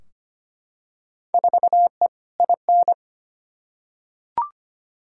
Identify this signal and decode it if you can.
Morse code, heard as a keyed tone, about 25 wpm, 700 Hz. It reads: 4E IN
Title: Field Day Exchanges - 25wpm